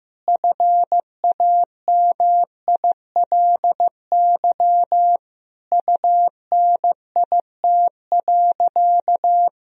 Morse code, heard as a keyed tone, about 15 wpm, 700 Hz. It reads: FAMILY UNIT.
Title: Sentences from Top 1000 Words - Review Part 2 - 15wpm